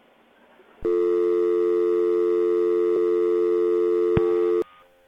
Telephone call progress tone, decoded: dial tone